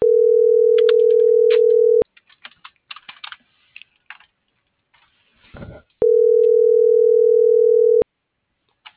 Ringback tone (North America), 440 + 480 Hz.